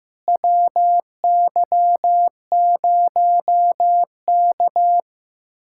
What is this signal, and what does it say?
Morse code, heard as a keyed tone, about 15 wpm, 700 Hz. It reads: WY0K
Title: USA Callsign DE Callsign - 4 Characters - 15wpm